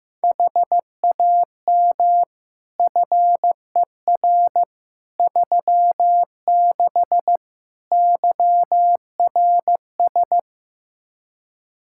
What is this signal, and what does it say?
Morse code, heard as a keyed tone, about 15 wpm, 700 Hz. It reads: HAM FER 36 YRS
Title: Practice QSO - Exchange 3 - More Info - 15wpm